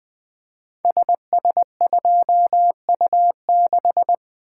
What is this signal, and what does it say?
Morse code, heard as a keyed tone, about 20 wpm, 700 Hz. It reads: SS2U6
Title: Code Groups - 5 Letter:Numbers - 20wpm